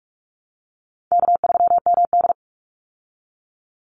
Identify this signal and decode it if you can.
Morse code, heard as a keyed tone, about 44 wpm, 700 Hz. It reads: K3MD
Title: Call Signs Repeated 3x - 45wpm